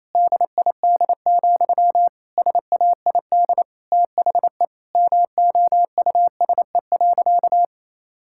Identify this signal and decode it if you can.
Morse code, heard as a keyed tone, about 28 wpm, 700 Hz. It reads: DID, SAID THE MOUSE.
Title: Sentences from Top 1000 Words - Review Part 5 - 28wpm